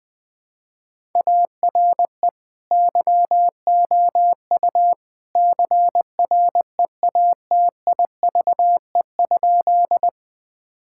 Morse code, heard as a keyed tone, about 20 wpm, 700 Hz. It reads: ARE YOU CREATIVE?
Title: Sentences from Top 1000 Words - Review Part 5 - 20wpm